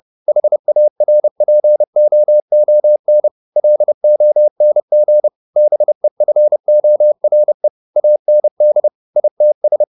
Morse code, heard as a keyed tone, about 30 wpm, 600 Hz. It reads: HARPOON LONG BEFORE AND ITS